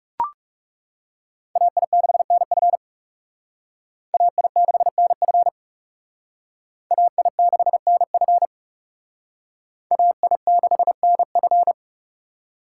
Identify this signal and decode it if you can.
Morse code, heard as a keyed tone, about 39 wpm, 700 Hz. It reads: AI6NF AI6NF AI6NF AI6NF